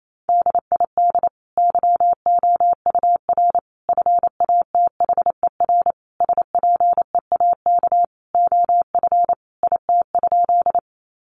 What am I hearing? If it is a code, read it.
Morse code, heard as a keyed tone, about 28 wpm, 700 Hz. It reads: DID YOUR FATHER SPEAK OF IT?
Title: Sentences from Top 1100 Words 28wpm